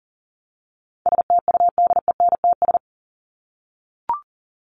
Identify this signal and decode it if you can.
Morse code, heard as a keyed tone, about 40 wpm, 700 Hz. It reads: STUDENTS
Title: ES:S Suffix - Top 100 Words with Base Words 40wpm